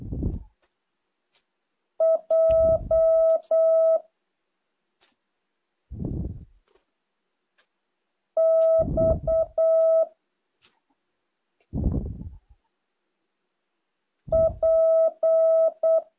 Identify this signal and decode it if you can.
Morse code, heard as a keyed tone, about 8 wpm, 650 Hz. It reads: J X P
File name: jh8cbh / 2023cw